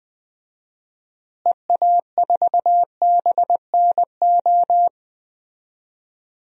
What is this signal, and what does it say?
Morse code, heard as a keyed tone, about 20 wpm, 700 Hz. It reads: EA4BNO
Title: DXCC-100 Call Signs - 6 Characters - Repeated 2x - 20wpm